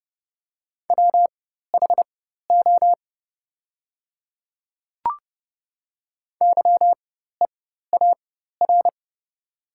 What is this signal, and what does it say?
Morse code, heard as a keyed tone, about 30 wpm, spaced as for 14 wpm, 700 Hz. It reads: WHO YEAR
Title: 4x Character Spacing - Top 200 Words - 30wpm